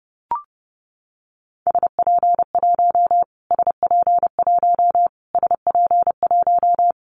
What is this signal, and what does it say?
Morse code, heard as a keyed tone, about 30 wpm, 700 Hz. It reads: SP1 SP1 SP1
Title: Callsign Prefixes for Top 25 DXCC - Repeated 3x - 30wpm